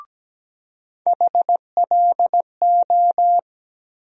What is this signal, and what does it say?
Morse code, heard as a keyed tone, about 17 wpm, 700 Hz. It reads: HLO
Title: USA Call Sign Postfixes - 3 Letters 17wpm